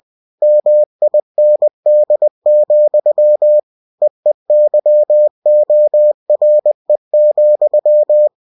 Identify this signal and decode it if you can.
Morse code, heard as a keyed tone, about 20 wpm, 600 Hz. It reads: MIND, EEYORE,
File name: cw-books / winnie-the-pooh / winnie-20wpm-page-0088